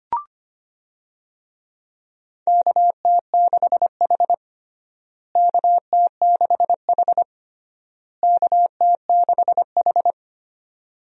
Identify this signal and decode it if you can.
Morse code, heard as a keyed tone, about 25 wpm, 700 Hz. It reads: KT6H KT6H KT6H
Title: USA Call Signs - 4 Characters - Repeated 3x - 25wpm